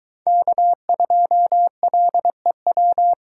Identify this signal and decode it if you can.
Morse code, heard as a keyed tone, about 23 wpm, 700 Hz. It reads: K2LEW